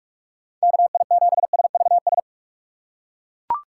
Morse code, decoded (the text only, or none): KI7SVS